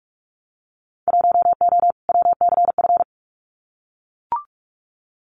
Morse code, heard as a keyed tone, about 44 wpm, 700 Hz. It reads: 1O WCF
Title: Winter Field Day Exchanges 45wpm